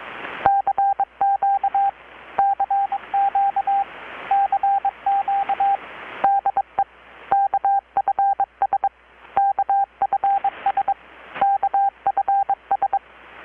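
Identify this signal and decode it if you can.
Morse code, heard as a keyed tone, about 22 wpm, 790 Hz, with radio noise. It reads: CQ CQ CQ DE KFS KFS KFS
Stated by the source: Morse from KPH's Night of Nights XX maritime radiotelegraphy event, received in Chiapas, Mexico about 3,800 km away on a Kenwood R-600 with a grounded 10 m long-wire antenna.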